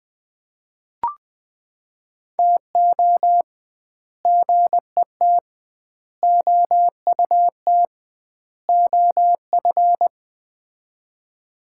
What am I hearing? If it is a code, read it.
Morse code, heard as a keyed tone, about 20 wpm, 700 Hz. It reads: TO GET OUT OF
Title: 2x Word-Spacing - Sets of 4 Words - 20wpm